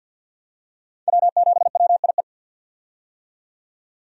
Morse code, heard as a keyed tone, about 46 wpm, 700 Hz. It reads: W7WIE
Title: USA Call Signs - 5 Characters - 50wpm